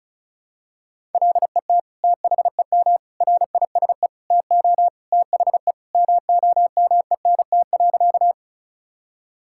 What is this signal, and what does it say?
Morse code, heard as a keyed tone, about 36 wpm, 700 Hz. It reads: LET THEM RISE TO THE MOMENT.
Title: Sentences from Top 1000 Words - Review Part 6 - 35wpm